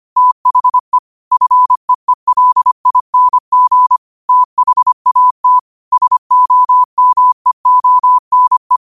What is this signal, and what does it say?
Morse code, heard as a keyed tone, about 25 wpm, 1000 Hz. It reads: THE FEELING THAT SOMEONE
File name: 25wpm